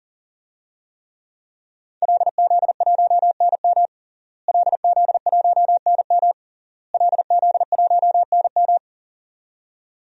Morse code, heard as a keyed tone, about 37 wpm, 700 Hz. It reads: LZ1NM LZ1NM LZ1NM
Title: DXCC-100 Call Signs - 5 Characters - Repeated 3x - 40wpm